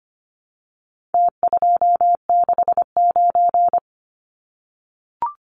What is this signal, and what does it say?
Morse code, heard as a keyed tone, about 25 wpm, 700 Hz. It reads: T269
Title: Numbers - Contest Style - 4-Digit 25wpm